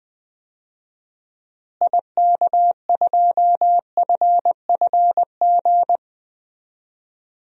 Morse code, heard as a keyed tone, about 20 wpm, 700 Hz. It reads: IK2FFG
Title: DXCC-100 Call Signs - 6 Characters - Repeated 2x - 20wpm